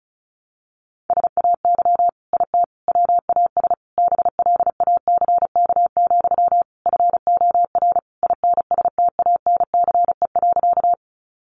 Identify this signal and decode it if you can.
Morse code, heard as a keyed tone, about 34 wpm, 700 Hz. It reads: SAY IT WAS BLACK, FOR INSTANCE.
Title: Sentences from Top 800 Words 35wpm